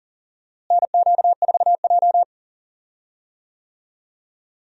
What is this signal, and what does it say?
Morse code, heard as a keyed tone, about 40 wpm, 700 Hz. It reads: NQ4J
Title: USA Call Signs - 4 Characters - Repeated 3x - 40wpm